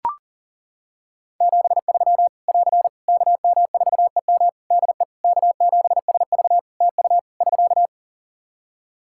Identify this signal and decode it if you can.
Morse code, heard as a keyed tone, about 37 wpm, 700 Hz. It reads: 73 <AR> KM4EM DE K7SV TU <SK>
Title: Practice QSO - Exchange 4 - Ending - 40wpm